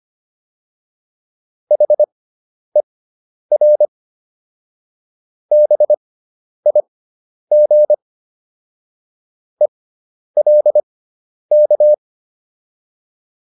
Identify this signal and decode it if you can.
Morse code, heard as a keyed tone, about 25 wpm, spaced as for 10 wpm, 600 Hz. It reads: HER BIG ELK